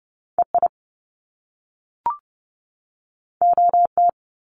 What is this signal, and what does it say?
Morse code, heard as a keyed tone, about 30 wpm, 700 Hz. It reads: EI OT
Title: Top Two Letter Combinations in Words 30wpm